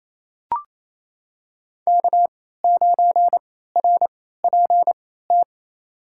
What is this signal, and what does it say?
Morse code, heard as a keyed tone, about 27 wpm, spaced as for 16 wpm, 700 Hz. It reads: K9RPT
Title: USA Call Signs - 3x Character Spacing - 28wpm